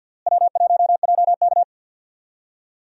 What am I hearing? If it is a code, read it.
Morse code, heard as a keyed tone, about 52 wpm, 700 Hz. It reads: W1JK
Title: USA Call Signs - 4 Characters - Repeated 3x - 50wpm